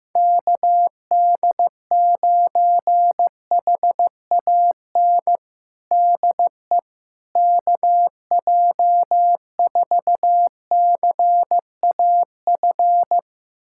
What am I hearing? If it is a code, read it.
Morse code, heard as a keyed tone, about 15 wpm, 700 Hz. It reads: KD9HAN DE KJ4CAF
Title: USA Callsign DE Callsign 15wpm